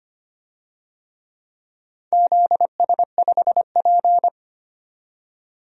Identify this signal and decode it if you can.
Morse code, heard as a keyed tone, about 25 wpm, 700 Hz. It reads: ZS5P
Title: DXCC-100 Call Signs - 4 Characters - 25wpm